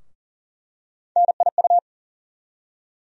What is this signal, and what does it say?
Morse code, heard as a keyed tone, about 40 wpm, 700 Hz. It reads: NIU
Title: USA Call Sign Postfixes - 3 Letters 40wpm